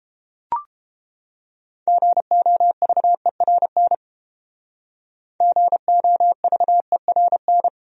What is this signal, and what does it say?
Morse code, heard as a keyed tone, about 30 wpm, 700 Hz. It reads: GOVERN GOVERN